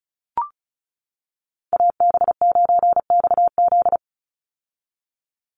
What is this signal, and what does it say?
Morse code, heard as a keyed tone, about 34 wpm, 700 Hz. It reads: AB9XZ
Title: USA Call Signs 35wpm